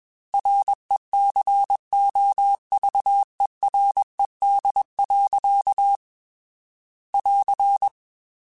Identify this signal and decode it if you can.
Morse code, heard as a keyed tone, about 21 wpm, 800 Hz. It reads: RECOVERED. <AR>
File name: Quotes / Quotes-mp3s / Character @20